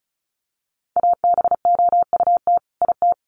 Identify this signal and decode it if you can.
Morse code, heard as a keyed tone, about 34 wpm, 700 Hz. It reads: ABOUT IT
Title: Sets of 2 Words 35wpm